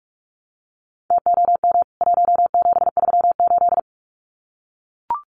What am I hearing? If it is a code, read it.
Morse code, heard as a keyed tone, about 44 wpm, 700 Hz. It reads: TOM 1738
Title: CWT Exchange 45wpm